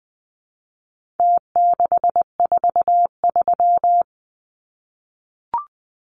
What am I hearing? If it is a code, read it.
Morse code, heard as a keyed tone, about 20 wpm, 700 Hz. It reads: T643
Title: Numbers - Contest Style - 4-Digit 20wpm